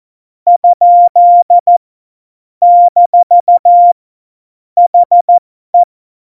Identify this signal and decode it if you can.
Morse code, heard as a keyed tone, about 14 wpm, spaced as for 12 wpm, 700 Hz. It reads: ? - HE
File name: Morsekurs / Witze